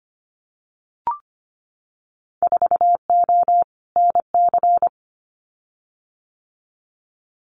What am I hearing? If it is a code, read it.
Morse code, heard as a keyed tone, about 25 wpm, 700 Hz. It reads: 4O NC
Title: Winter Field Day Exchanges 25wpm